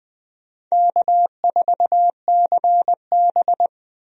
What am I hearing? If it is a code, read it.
Morse code, heard as a keyed tone, about 20 wpm, 700 Hz. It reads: K4CB
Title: USA Call Signs - 4 Characters - 20wpm